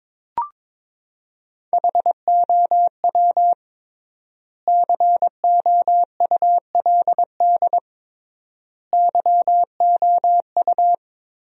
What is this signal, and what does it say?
Morse code, heard as a keyed tone, about 22 wpm, 700 Hz. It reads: HOW COULD YOU
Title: 3x Word-Spacing - Sentences from Top 300 Words - 22wpm